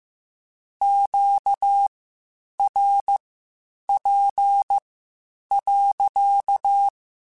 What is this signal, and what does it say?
Morse code, heard as a keyed tone, about 15 wpm, spaced as for 8 wpm, 800 Hz. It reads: QRP.
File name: Q Calls / Q Calls-mp3s / Q Calls sppc @13